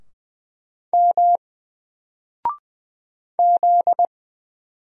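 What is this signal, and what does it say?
Morse code, heard as a keyed tone, about 20 wpm, 700 Hz. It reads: M Z
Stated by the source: Single letter and number Morse drill at 20 wpm, from N4LCD.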